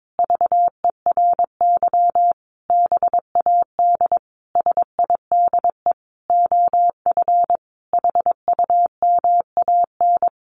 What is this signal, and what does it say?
Morse code, heard as a keyed tone, about 22 wpm, 700 Hz. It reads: VERY BAD SIDE OF HUMAN
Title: Sentences from Top 700 Words 22wpm